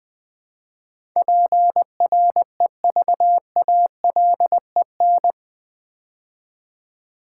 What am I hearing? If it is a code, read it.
Morse code, heard as a keyed tone, about 20 wpm, 700 Hz. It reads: PREVALEN
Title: Build A Word - 4 Letters At A Time - 10-Letter Words - 20wpm